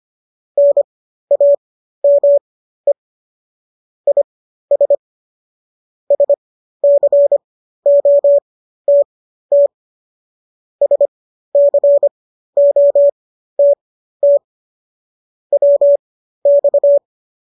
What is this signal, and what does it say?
Morse code, heard as a keyed tone, about 25 wpm, spaced as for 13 wpm, 570 Hz. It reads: NAME IS SCOTT SCOTT WX